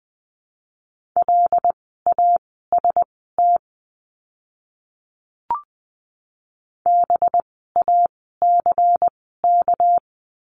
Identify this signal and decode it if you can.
Morse code, heard as a keyed tone, about 20 wpm, spaced as for 15 wpm, 700 Hz. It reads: LAST BACK
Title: 244 Lesson 39f - Letter z - Words 20wpm